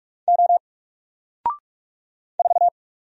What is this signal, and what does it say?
Morse code, heard as a keyed tone, about 44 wpm, 700 Hz. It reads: O 4
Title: Single Letter-Number 45wpm